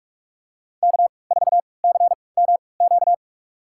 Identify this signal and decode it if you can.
Morse code, heard as a keyed tone, about 46 wpm, spaced as for 25 wpm, 700 Hz. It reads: K4CMQ
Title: USA Call Signs - 3x Character Spacing - 45wpm